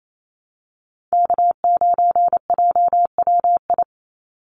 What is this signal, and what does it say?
Morse code, heard as a keyed tone, about 28 wpm, 700 Hz. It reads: K9JWI
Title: Call Signs Repeated 2x - 28wpm